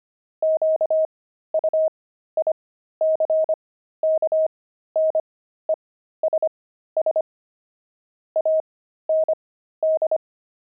Morse code, heard as a keyed tone, about 25 wpm, spaced as for 13 wpm, 630 Hz. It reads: QUICKNESS AND
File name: pangram-fs-13-cs-25-dexterity